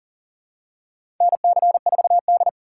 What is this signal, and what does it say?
Morse code, heard as a keyed tone, about 40 wpm, 700 Hz. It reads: NC4D
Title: USA Call Signs - 4 Characters - 2x Character Spacing - 40wpm